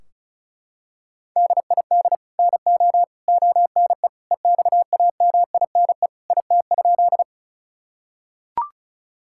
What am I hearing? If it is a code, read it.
Morse code, heard as a keyed tone, about 36 wpm, 700 Hz. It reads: DID NO ONE EXAMINE IT?
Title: Sentences from Top 1600 Words 35wpm